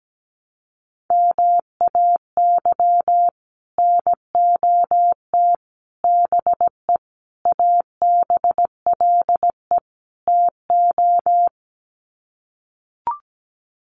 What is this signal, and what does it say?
Morse code, heard as a keyed tone, about 17 wpm, 700 Hz. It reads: MAY NOT BE ABLE TO
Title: Sets of 5 Words 17wpm